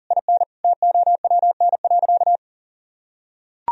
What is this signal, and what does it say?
Morse code, heard as a keyed tone, about 37 wpm, 700 Hz. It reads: IN TOWN.
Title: Sentences from Top 1000 Words - Review Part 2 - 40wpm